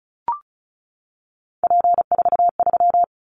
Morse code, heard as a keyed tone, about 34 wpm, 700 Hz. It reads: P43
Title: Callsign Prefixes for Top 100 DXCC - 35wpm